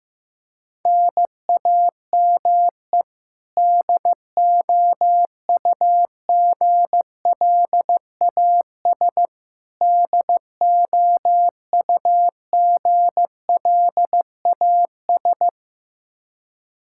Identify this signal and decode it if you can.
Morse code, heard as a keyed tone, about 15 wpm, 700 Hz. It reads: NAME DOUGLAS DOUGLAS
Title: Practice QSO - Exchange 1 - Beginning - 15wpm